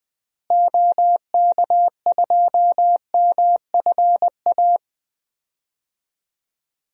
Morse code, heard as a keyed tone, about 20 wpm, 700 Hz. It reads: OK2MFA
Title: DXCC-100 Call Signs - 6 Characters - Repeated 3x - 20wpm